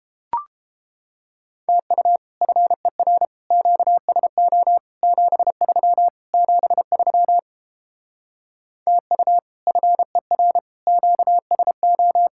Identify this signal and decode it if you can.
Morse code, heard as a keyed tone, about 29 wpm, 700 Hz. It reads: TU FER QSO 73 73 TU FER QSO